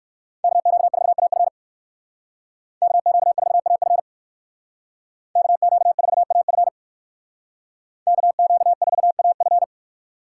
Morse code, heard as a keyed tone, about 58 wpm, 700 Hz. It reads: KQ4AF KQ4AF KQ4AF KQ4AF